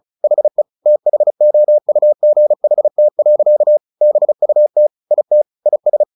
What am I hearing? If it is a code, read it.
Morse code, heard as a keyed tone, about 34 wpm, 600 Hz. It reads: HE THOUGHT. BUT IT IS
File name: the old man and the sea / the old man and the sea 1 35wpm